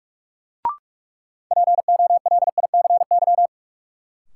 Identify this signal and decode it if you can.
Morse code, heard as a keyed tone, about 46 wpm, 700 Hz. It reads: POLICY
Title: Sentences from Top 1000 Words - Review Part 5 - New Word Drill - 45wpm